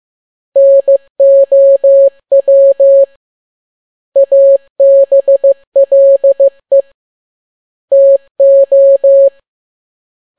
Morse code, heard as a keyed tone, about 15 wpm, 550 Hz. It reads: NOW ABLE TO
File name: n6tt p1 15C13